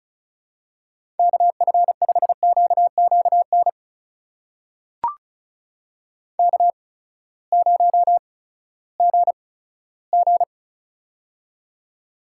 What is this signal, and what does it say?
Morse code, heard as a keyed tone, about 36 wpm, spaced as for 18 wpm, 700 Hz. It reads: KF5QQN K0GG
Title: USA Call Signs - 8x Character Spacing - 35wpm